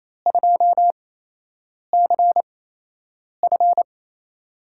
Morse code, heard as a keyed tone, about 28 wpm, spaced as for 8 wpm, 700 Hz. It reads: 2CF
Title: USA Call Signs - 5 Characters - 8x Character Spacing - 28wpm